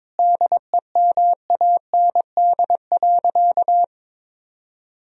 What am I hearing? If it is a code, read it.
Morse code, heard as a keyed tone, about 22 wpm, 700 Hz. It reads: DEMAND.
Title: Sentences from Top 1000 Words - Review Part 4 - 22wpm